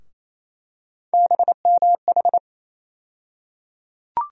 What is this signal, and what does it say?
Morse code, heard as a keyed tone, about 28 wpm, 700 Hz. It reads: BMH
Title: USA Call Sign Postfixes - 3 Letters 28wpm